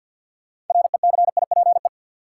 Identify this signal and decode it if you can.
Morse code, heard as a keyed tone, about 53 wpm, 700 Hz. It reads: RECIPE
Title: Sentences from Top 1200 Words - New Word Drill Alt - 50wpm